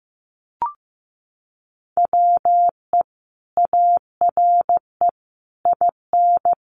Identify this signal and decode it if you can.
Morse code, heard as a keyed tone, about 15 wpm, 700 Hz. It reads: WE ARE IN
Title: Sentences from Top 1400 Words 15wpm